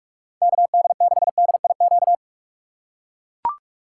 Morse code, heard as a keyed tone, about 44 wpm, 700 Hz. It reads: KD6DIQ